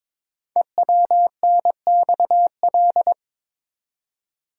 Morse code, heard as a keyed tone, about 22 wpm, 700 Hz. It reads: EWNXL
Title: Code Groups - 5 Letters - 22wpm